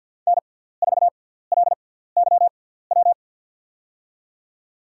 Morse code, heard as a keyed tone, about 50 wpm, spaced as for 17 wpm, 700 Hz. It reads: N4LYW
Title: USA Call Signs - 5 Characters - 6x Character Spacing - 50wpm